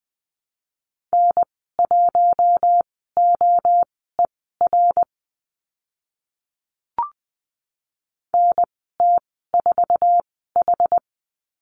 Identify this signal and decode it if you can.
Morse code, heard as a keyed tone, about 20 wpm, spaced as for 14 wpm, 700 Hz. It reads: N1OER NT4H
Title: 062 Lesson 11g - Letter h - Call Signs 20wpm